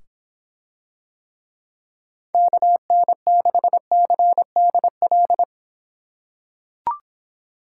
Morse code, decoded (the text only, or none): KN6CDL